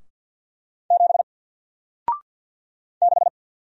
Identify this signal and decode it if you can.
Morse code, heard as a keyed tone, about 50 wpm, 700 Hz. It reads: 7 6
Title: Single Letter-Number Alt 50wpm